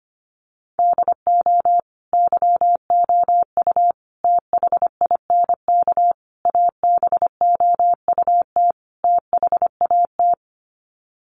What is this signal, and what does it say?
Morse code, heard as a keyed tone, about 25 wpm, 700 Hz. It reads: DO YOU THINK ABOUT THAT
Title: Sets of 5 Words Encore2 25wpm